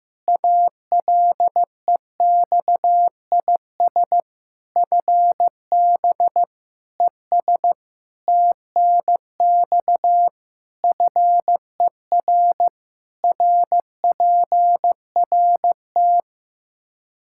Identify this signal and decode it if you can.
Morse code, heard as a keyed tone, about 15 wpm, 700 Hz. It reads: ALEXIS FB ES TNX FER RPRT
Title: Practice QSO - Exchange 2 - Info - 15wpm